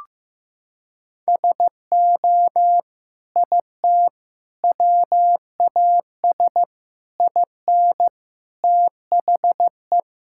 Morse code, heard as a keyed tone, about 16 wpm, 700 Hz. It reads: SO IT WAS IN THE
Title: Sentences from Top 1000 Words - Review Part 5 - 15wpm